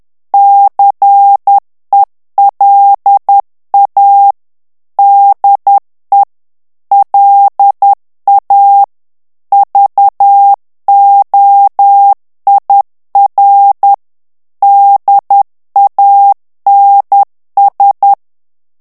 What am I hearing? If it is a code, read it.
Morse code, heard as a keyed tone, about 10 wpm, 800 Hz. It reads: CELA DE LA VOIR DANS
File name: FRANCAIS TEXTE CLAIR 2 VITESSE 10 MOTS MINUTES